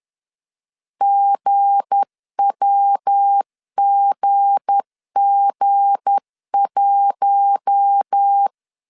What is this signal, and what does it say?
Morse code, heard as a keyed tone, about 10 wpm, 800 Hz. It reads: GWGG1
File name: LEZIONI / 7 GF9Z / 7 G 10wpm 2501